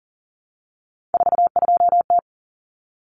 Morse code, heard as a keyed tone, about 40 wpm, 700 Hz. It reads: 42T
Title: Common Cut Numbers - 3-Digit 40wpm